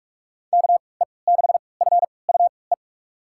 Morse code, heard as a keyed tone, about 46 wpm, spaced as for 26 wpm, 700 Hz. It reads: KE6FUE
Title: USA Call Signs - 6 Characters - 3x Character Spacing - 45wpm